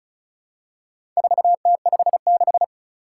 Morse code, heard as a keyed tone, about 36 wpm, 700 Hz. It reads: 4T56
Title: Four Challenging Letter-Number Combinations 35wpm